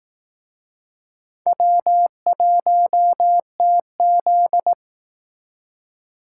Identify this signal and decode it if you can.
Morse code, heard as a keyed tone, about 18 wpm, 700 Hz. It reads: W1TZ
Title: Speed Racing - USA 1x2 Call Signs 15wpm